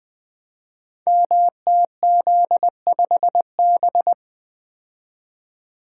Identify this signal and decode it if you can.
Morse code, heard as a keyed tone, about 21 wpm, 700 Hz. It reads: MTZ5B
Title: Code Groups - 5 Letter:Numbers - 20wpm